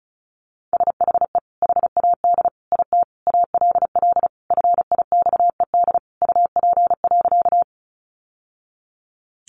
Morse code, heard as a keyed tone, about 34 wpm, 700 Hz. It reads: SHE HAD IT ALL FIXED UP.
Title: Sentences from Top 1100 Words 35wpm